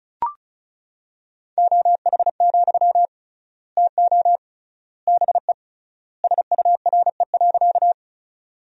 Morse code, heard as a keyed tone, about 36 wpm, 700 Hz. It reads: OH, TO BE SURE.
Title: 3x Word-Spacing - Sentences from Top 500 Words - 35wpm